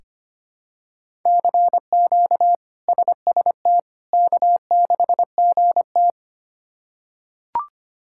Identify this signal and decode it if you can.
Morse code, heard as a keyed tone, about 25 wpm, 700 Hz. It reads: CQ SST K6GT
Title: K1USN SST QSO Practice - 25wpm